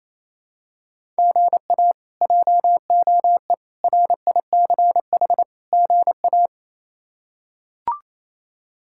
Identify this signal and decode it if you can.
Morse code, heard as a keyed tone, about 28 wpm, 700 Hz. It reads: GA JOE RICH GA
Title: K1USN SST QSO Practice - 28wpm